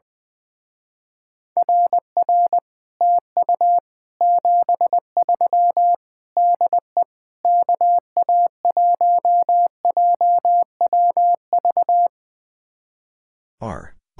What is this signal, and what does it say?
Morse code, heard as a keyed tone, about 20 wpm, 700 Hz. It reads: RR TU 73 DE KA1JWV
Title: Speed-Racing Common QSO Phrases - 20wpm